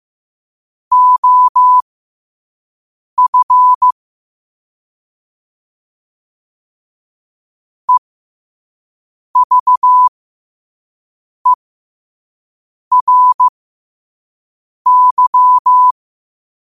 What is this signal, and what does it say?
Morse code, heard as a keyed tone, about 15 wpm, spaced as for 5 wpm, 1000 Hz. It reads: OF EVERY
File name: 5wpm